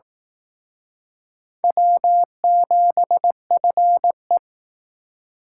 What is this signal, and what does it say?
Morse code, heard as a keyed tone, about 18 wpm, 700 Hz. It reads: W7FE
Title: Speed Racing - USA 1x2 Call Signs 15wpm